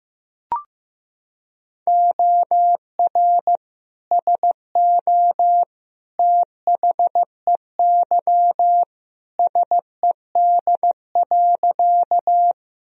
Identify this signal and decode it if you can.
Morse code, heard as a keyed tone, about 16 wpm, 700 Hz. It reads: OR SO THEY SED.
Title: Sentences from Top 100 Words and Abbreviations 15wpm